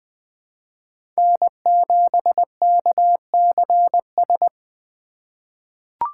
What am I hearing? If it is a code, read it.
Morse code, heard as a keyed tone, about 20 wpm, 700 Hz. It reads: N7KCS
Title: USA Call Signs - 5 Characters - 3x Character Spacing - 20wpm